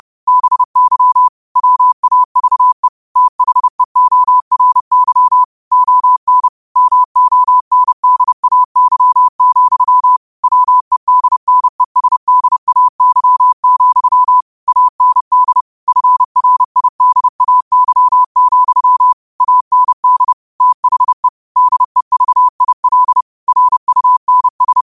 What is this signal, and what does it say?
Morse code, heard as a keyed tone, about 30 wpm, 1000 Hz. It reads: BY WAVE THEORY ON MONDAY, WEDNESDAY, AND FRIDAY, AND THE DEVIL RUNS